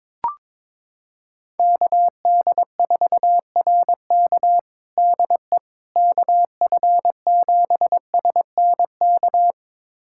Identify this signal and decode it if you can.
Morse code, heard as a keyed tone, about 22 wpm, 700 Hz. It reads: KD4RK DE KF7SNK
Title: USA Callsign DE Callsign 22wpm